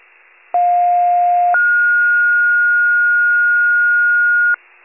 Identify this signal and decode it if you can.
Two-tone page, heard as a one-second tone, then a three-second tone, about 710 then 1430 Hz. It is heard from 0.5 to 4.5 s.
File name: wcalert